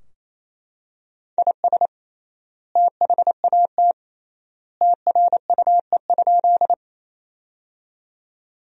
Morse code, heard as a keyed tone, about 28 wpm, 700 Hz. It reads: IS THAT TRUE?
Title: 3x Word-Spacing - Sentences from Top 400 Words - 28wpm